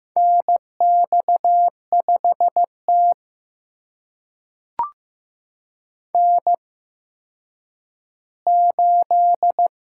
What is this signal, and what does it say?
Morse code, heard as a keyed tone, about 15 wpm, spaced as for 9 wpm, 700 Hz. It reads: NX5T N8
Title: USA Call Signs - 4 Characters - 8x Character Spacing - 15wpm